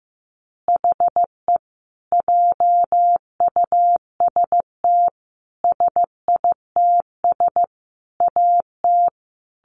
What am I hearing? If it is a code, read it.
Morse code, heard as a keyed tone, about 15 wpm, 700 Hz. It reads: HE JUST SITS AT
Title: Additional Easy Phrases 15wpm